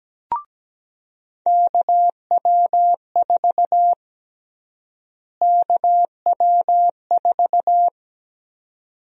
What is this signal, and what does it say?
Morse code, heard as a keyed tone, about 17 wpm, 700 Hz. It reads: KW4 KW4
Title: USA Call Sign Prefixes - Repeated 2x - 17wpm